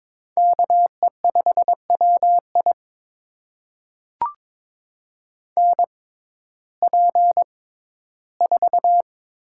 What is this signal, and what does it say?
Morse code, heard as a keyed tone, about 22 wpm, spaced as for 14 wpm, 700 Hz. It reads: KE5WI NP4
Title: USA Call Signs - 5 Characters - 6x Character Spacing - 22wpm